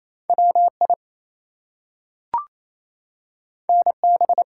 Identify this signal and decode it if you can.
Morse code, heard as a keyed tone, about 28 wpm, 700 Hz. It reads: WI NB
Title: US States and Canadian Provinces 28wpm